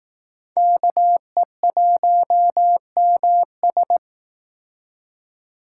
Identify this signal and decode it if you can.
Morse code, heard as a keyed tone, about 19 wpm, 700 Hz. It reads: KE1MS